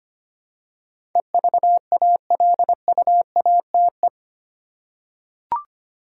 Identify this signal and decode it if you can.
Morse code, heard as a keyed tone, about 25 wpm, 700 Hz. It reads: EVALUATE
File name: 8 Letter Words - First 3 Letters Given - 25wpm